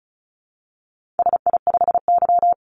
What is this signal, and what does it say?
Morse code, heard as a keyed tone, about 34 wpm, 700 Hz. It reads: SI5Y
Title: Call Signs - 35wpm